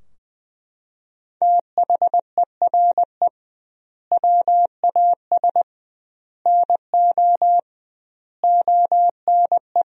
Morse code, heard as a keyed tone, about 20 wpm, 700 Hz. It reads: THERE WAS NO ONE
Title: 2x Word-Spacing - Sentences from Top 200 Words - 20wpm